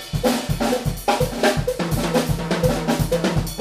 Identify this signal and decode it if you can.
There is no code.